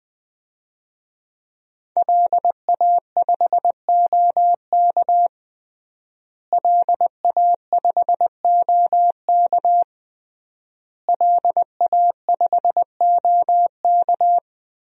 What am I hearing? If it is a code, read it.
Morse code, heard as a keyed tone, about 20 wpm, 700 Hz. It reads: LA5OK LA5OK LA5OK
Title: DXCC-100 Call Signs - 5 Characters - Repeated 3x - 20wpm